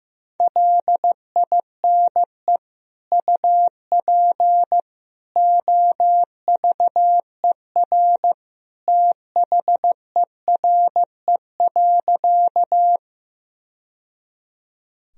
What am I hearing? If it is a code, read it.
Morse code, heard as a keyed tone, about 15 wpm, 700 Hz. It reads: LINE UP OVER THERE.
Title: Sentences from Top 1000 Words - Review Part 1 - 15wpm